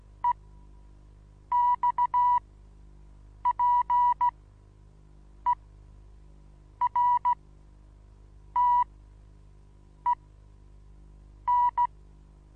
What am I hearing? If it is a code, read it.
Morse code, heard as a keyed tone, about 15 wpm, spaced as for 6 wpm, 980 Hz. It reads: EXPERTEN
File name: Onlinelehrgang / morsen / DJ4UF Morsen25